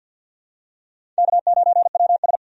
Morse code, heard as a keyed tone, about 47 wpm, 700 Hz. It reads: K9WS
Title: USA Call Signs - 4 Characters - 50wpm